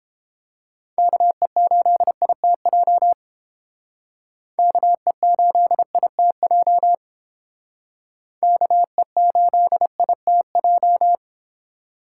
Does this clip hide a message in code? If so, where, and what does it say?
Yes: Morse code from 1.0 to 11.2 s, heard as a keyed tone, about 28 wpm, 700 Hz. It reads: KE8ITJ KE8ITJ KE8ITJ